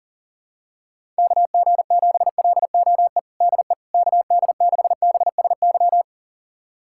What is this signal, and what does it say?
Morse code, heard as a keyed tone, about 40 wpm, 700 Hz. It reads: KG7LOE DE KD6BSY
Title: USA Callsign DE Callsign - 6 Characters - 40wpm